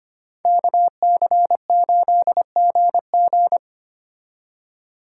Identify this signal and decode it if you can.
Morse code, heard as a keyed tone, about 29 wpm, 700 Hz. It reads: KC8GG
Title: Speed Racing - USA 2x2 Call Signs 25wpm